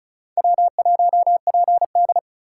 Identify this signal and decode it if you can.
Morse code, heard as a keyed tone, about 45 wpm, 700 Hz. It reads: W1PD